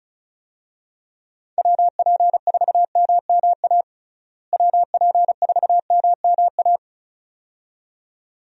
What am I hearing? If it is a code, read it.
Morse code, heard as a keyed tone, about 34 wpm, 700 Hz. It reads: WP4MMA WP4MMA
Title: USA Call Signs - 6 Characters - Repeated 2x - 35wpm